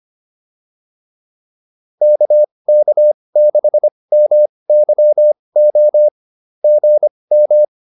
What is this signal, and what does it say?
Morse code, heard as a keyed tone, about 25 wpm, spaced as for 20 wpm, 590 Hz. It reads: KK6MYO GM